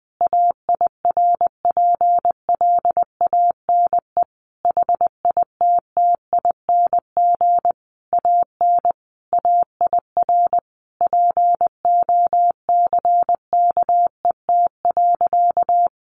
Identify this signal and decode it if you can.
Morse code, heard as a keyed tone, about 20 wpm, 700 Hz. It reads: AIRPLANE HITTING AN AIR POCKET.
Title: A Sound Of Thunder 20wpm